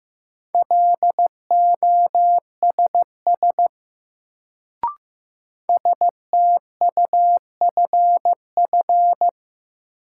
Morse code, heard as a keyed tone, about 15 wpm, 700 Hz. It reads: LOSS STUFF
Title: Sentences from Top 1400 Words - New Word Drill Alt - 15wpm